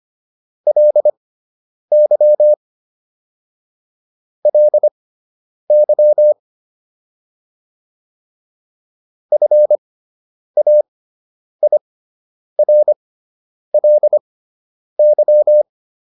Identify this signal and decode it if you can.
Morse code, heard as a keyed tone, about 25 wpm, spaced as for 9 wpm, 600 Hz. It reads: LY LY FAIRLY